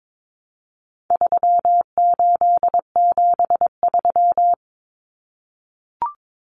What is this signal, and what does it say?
Morse code, heard as a keyed tone, about 22 wpm, 700 Hz. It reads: 3873
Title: Numbers - 4-Digit 22wpm